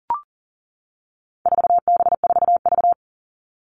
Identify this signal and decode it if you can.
Morse code, heard as a keyed tone, about 40 wpm, 700 Hz. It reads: 4B4V
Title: All Cut Numbers - 4-Digit 40wpm